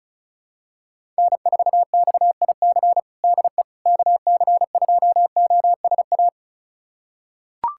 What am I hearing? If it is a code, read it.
Morse code, heard as a keyed tone, about 34 wpm, 700 Hz. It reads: N4XIC DE KC2OSA
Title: USA Callsign DE Callsign 35wpm